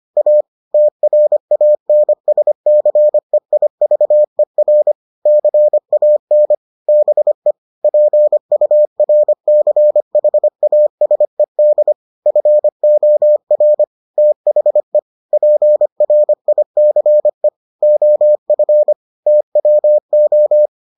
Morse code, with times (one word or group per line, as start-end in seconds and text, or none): 0.2-0.4 A
0.7-4.9 TRANSCEIVER
5.3-6.5 CAN
6.9-7.5 BE
7.8-11.9 PURCHASED
12.3-13.8 FOR
14.2-15.0 THE
15.3-17.5 PRICE
17.8-18.9 OF
19.3-20.7 TWO